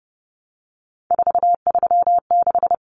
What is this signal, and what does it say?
Morse code, heard as a keyed tone, about 30 wpm, 700 Hz. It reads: 436
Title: Common Cut Numbers - 3-Digit 30wpm